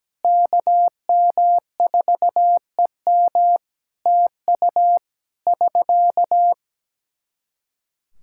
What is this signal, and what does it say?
Morse code, heard as a keyed tone, about 17 wpm, 700 Hz. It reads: KM4EM TU <SK>
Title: Practice QSO - Exchange 4 - Ending - 17wpm